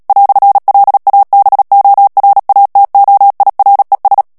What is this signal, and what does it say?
Morse code, heard as a keyed tone, about 37 wpm, 800 Hz. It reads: "LABORATOIRES
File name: CW / FICHIERS-MP3 / FRANCAIS TEXTE CLAIR 1 VITESSE 35 MOTS MINUTES